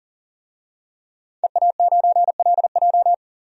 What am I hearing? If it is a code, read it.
Morse code, heard as a keyed tone, about 40 wpm, 700 Hz. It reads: EA9LJ
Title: DXCC-100 Call Signs - 5 Characters - 40wpm